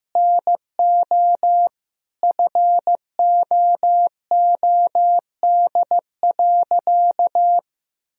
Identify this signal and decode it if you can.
Morse code, heard as a keyed tone, about 15 wpm, 700 Hz. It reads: NO FOOD.
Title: Sentences from Top 1000 Words - Review Part 1 - 15wpm